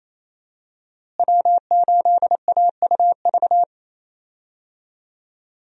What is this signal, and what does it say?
Morse code, heard as a keyed tone, about 28 wpm, 700 Hz. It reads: W8AUV
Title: USA Call Signs - 5 Characters - Repeated 2x - 28wpm